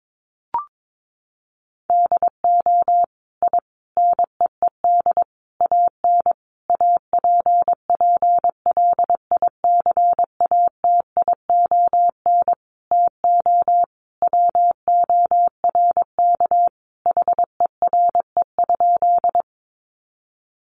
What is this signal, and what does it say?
Morse code, heard as a keyed tone, about 22 wpm, 700 Hz. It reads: DO I NEED AN APPLICATION TO WORK HERE?
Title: Sentences from Top 1200 Words 22wpm